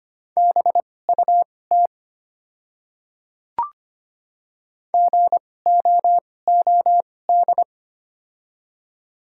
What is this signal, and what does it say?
Morse code, heard as a keyed tone, about 25 wpm, spaced as for 18 wpm, 700 Hz. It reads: BUT GOOD
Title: 2x Character Spacing - Top 300 Words - 25wpm